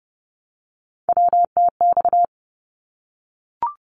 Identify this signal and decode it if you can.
Morse code, heard as a keyed tone, about 30 wpm, 700 Hz. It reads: WTX
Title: ARRL Sections 30wpm